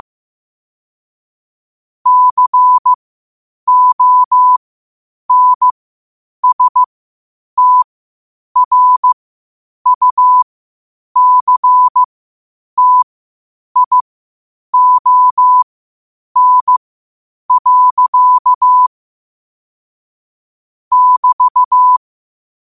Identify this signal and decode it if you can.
Morse code, heard as a keyed tone, about 15 wpm, spaced as for 8 wpm, 1000 Hz. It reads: CONSTRUCTION. <BT>
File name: qotdfiles / 8wpm